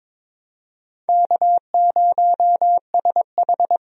Morse code, heard as a keyed tone, about 22 wpm, 700 Hz. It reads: K0SH
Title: USA Call Signs - 4 Characters - 22wpm